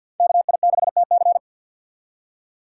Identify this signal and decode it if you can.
Morse code, heard as a keyed tone, about 47 wpm, 700 Hz. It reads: KI6TC
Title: USA Call Signs - 5 Characters - 4x Character Spacing - 50wpm